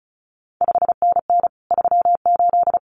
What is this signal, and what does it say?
Morse code, heard as a keyed tone, about 36 wpm, 700 Hz. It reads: 5NN 38
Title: CQ WW DX Contest Exchange Encore - 35wpm